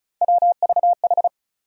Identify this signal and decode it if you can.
Morse code, heard as a keyed tone, about 36 wpm, 700 Hz. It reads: WVH
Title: USA Call Sign Postfixes - 3 Letters 35wpm